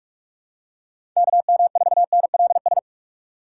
Morse code, heard as a keyed tone, about 44 wpm, 700 Hz. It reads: KM4NLS
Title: USA Call Signs - 6 Characters - 4x Character Spacing - 45wpm